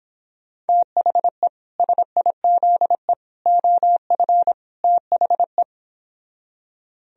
Morse code, heard as a keyed tone, about 29 wpm, 700 Hz. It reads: THE SIZE OF THE